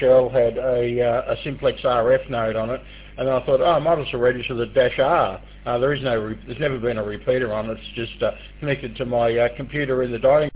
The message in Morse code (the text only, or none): none